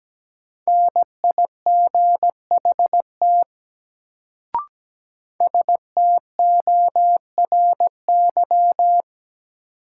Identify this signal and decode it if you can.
Morse code, heard as a keyed tone, about 17 wpm, 700 Hz. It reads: NIGHT STORY
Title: Sentences from Top 300 Words - New Word Drill - 17wpm